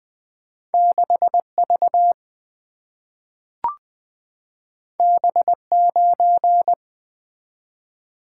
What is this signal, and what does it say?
Morse code, heard as a keyed tone, about 20 wpm, 700 Hz. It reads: 6 V B 9
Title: Single-Letter Single-Number 20wpm